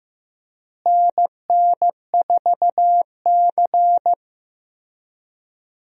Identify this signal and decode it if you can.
Morse code, heard as a keyed tone, about 15 wpm, 700 Hz. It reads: NN4C
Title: USA Call Signs - 4 Characters - Repeated 3x - 15wpm